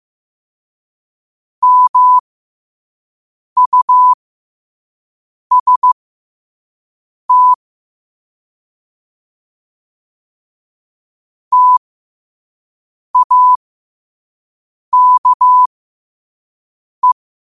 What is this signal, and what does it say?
Morse code, heard as a keyed tone, about 15 wpm, spaced as for 5 wpm, 1000 Hz. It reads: MUST TAKE
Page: New quotes every day in morse code at 5 Words per minute.